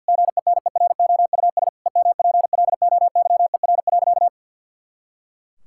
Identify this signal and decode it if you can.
Morse code, heard as a keyed tone, about 49 wpm, 700 Hz. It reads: GENEROUS EMPLOYER.
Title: Pangram Challenge v2 - 50wpm